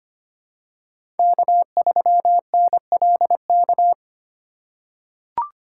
Morse code, heard as a keyed tone, about 25 wpm, 700 Hz. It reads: K3NLK
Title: USA Call Signs - 3x Character Spacing - 25wpm